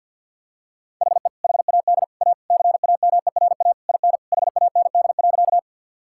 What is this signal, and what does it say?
Morse code, heard as a keyed tone, about 50 wpm, 700 Hz. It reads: HE HAD A CAMERA IN HAND.
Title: Sentences from Top 1000 Words - Review Part 5 - 50wpm